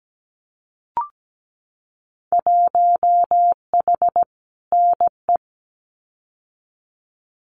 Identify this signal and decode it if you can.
Morse code, heard as a keyed tone, about 17 wpm, 700 Hz. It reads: 1H NE
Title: Winter Field Day Exchanges Encore 17wpm